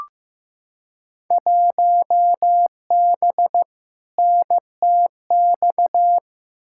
Morse code, heard as a keyed tone, about 15 wpm, 700 Hz. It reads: 1B NTX
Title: Field Day Exchanges - 15wpm